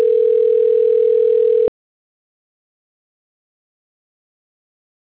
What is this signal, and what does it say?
Ringback tone (North America), 440 + 480 Hz.
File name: rg-1600-unknown- - - .19321